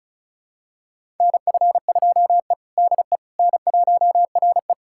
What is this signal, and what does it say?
Morse code, heard as a keyed tone, about 36 wpm, 700 Hz. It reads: NF2E DE N1RE
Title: USA Callsign DE Callsign - 4 Characters - 35wpm